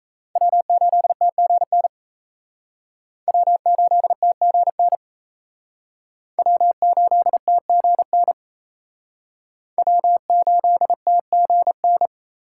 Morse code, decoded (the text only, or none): W8TGN W8TGN W8TGN W8TGN